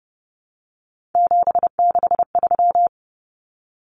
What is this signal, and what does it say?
Morse code, heard as a keyed tone, about 30 wpm, 700 Hz. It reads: 763
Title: Numbers - 3-Digit 30wpm